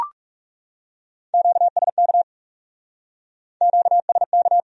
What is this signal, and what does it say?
Morse code, heard as a keyed tone, about 39 wpm, 700 Hz. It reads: QSK QSK